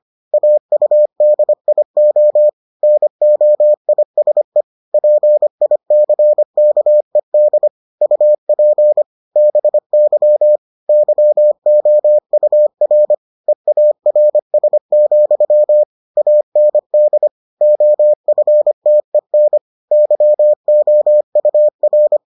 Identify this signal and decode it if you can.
Morse code, heard as a keyed tone, about 25 wpm, 600 Hz. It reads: AUDIO NOISE PICKED UP BY YOUR EARS, AND OFTEN YOUR